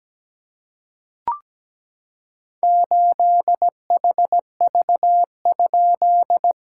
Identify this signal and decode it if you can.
Morse code, heard as a keyed tone, about 17 wpm, 700 Hz. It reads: 8HV?
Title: SOTA QSO Practice 17wpm